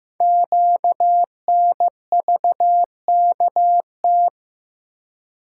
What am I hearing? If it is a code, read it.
Morse code, heard as a keyed tone, about 15 wpm, 700 Hz. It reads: QNVKT
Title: Code Groups - 5 Letter:Numbers - 15wpm